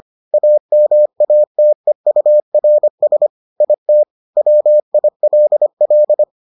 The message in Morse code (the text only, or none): AMATEURS IT WILL